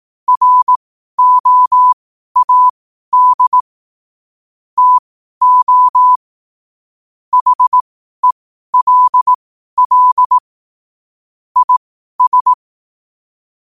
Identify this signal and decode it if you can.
Morse code, heard as a keyed tone, about 18 wpm, spaced as for 12 wpm, 1000 Hz. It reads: ROAD TO HELL IS
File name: qotdfiles / 12wpm